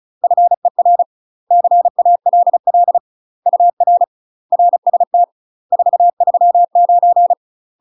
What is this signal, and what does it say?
Morse code, heard as a keyed tone, about 35 wpm, 700 Hz. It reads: FER CALL UR RST 439
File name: UK MASTER QSOs TRAINING / 35WPM / UK MASTER QSOs 0029